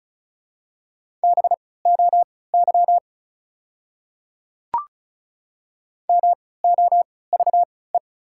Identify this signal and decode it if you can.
Morse code, heard as a keyed tone, about 36 wpm, spaced as for 20 wpm, 700 Hz. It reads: BOY MOVE
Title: 3x Character Spacing - Top 200 Words - 35wpm